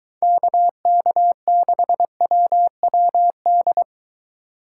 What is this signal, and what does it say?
Morse code, heard as a keyed tone, about 22 wpm, 700 Hz. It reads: KK6WWD